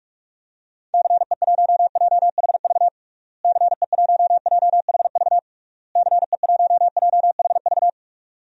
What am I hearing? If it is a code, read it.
Morse code, heard as a keyed tone, about 41 wpm, 700 Hz. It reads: CE1JHV CE1JHV CE1JHV
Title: DXCC-100 Call Signs - 6 Characters - Repeated 3x - 45wpm